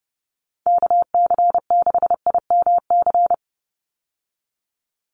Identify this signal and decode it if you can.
Morse code, heard as a keyed tone, about 30 wpm, 700 Hz. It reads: KC6IMC
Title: USA Call Signs 30wpm